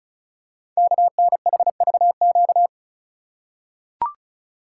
Morse code, heard as a keyed tone, about 36 wpm, 690 Hz. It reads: KNHVQ
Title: Code Groups - 5 Letters - 35wpm